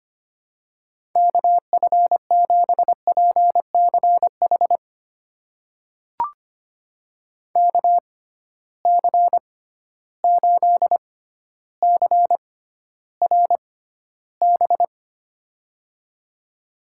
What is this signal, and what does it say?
Morse code, heard as a keyed tone, about 25 wpm, spaced as for 13 wpm, 700 Hz. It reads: KF7PCH KC8CRB
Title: USA Call Signs - 6 Characters - 6x Character Spacing - 25wpm